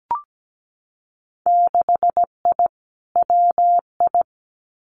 Morse code, heard as a keyed tone, about 17 wpm, 700 Hz. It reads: 6I WI
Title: Winter Field Day Exchanges Encore 17wpm